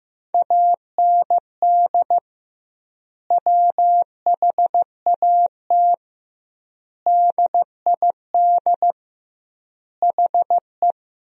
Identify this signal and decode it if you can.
Morse code, heard as a keyed tone, about 16 wpm, 700 Hz. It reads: AND WHAT DID HE
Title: 2x Word-Spacing - Sentences from Top 200 Words - 15wpm